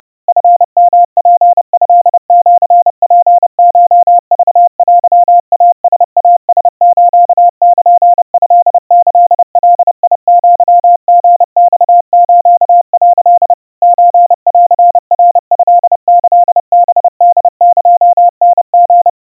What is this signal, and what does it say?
Morse code (Japanese Wabun code), heard as a keyed tone, about 30 wpm, 700 Hz. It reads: チヨツトシツコクテイライラスルトキガアリマス」 ソンナトキハホエタリ
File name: kawa / Morse sample / kamyu 110 cpm 700 Hz Crispy